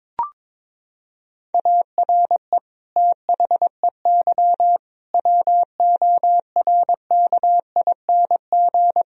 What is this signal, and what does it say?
Morse code, heard as a keyed tone, about 22 wpm, 700 Hz. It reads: ARE THEY WORKING
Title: Sentences from Top 1000 Words - Review Part 4 - 22wpm